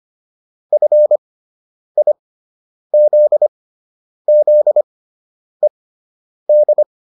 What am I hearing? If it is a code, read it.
Morse code, heard as a keyed tone, about 25 wpm, spaced as for 9 wpm, 600 Hz. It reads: FIZZED